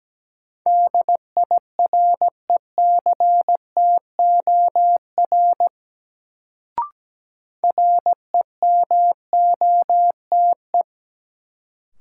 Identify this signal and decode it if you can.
Morse code, heard as a keyed tone, about 17 wpm, 700 Hz. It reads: DIRECTOR REMOTE
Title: Sentences from Top 1300 Words - New Word Drill - 17wpm